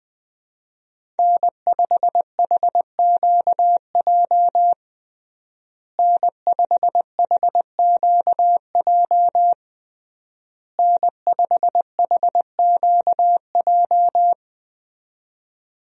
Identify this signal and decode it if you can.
Morse code, heard as a keyed tone, about 20 wpm, 700 Hz. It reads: N5HQJ N5HQJ N5HQJ
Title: USA Call Signs - 5 Characters - Repeated 3x - 20wpm